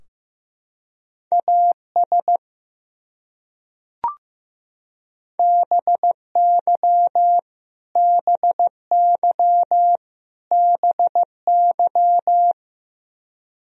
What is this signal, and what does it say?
Morse code, heard as a keyed tone, about 15 wpm, 700 Hz. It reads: AS BY BY BY
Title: Top 100 Words Repeated 15wpm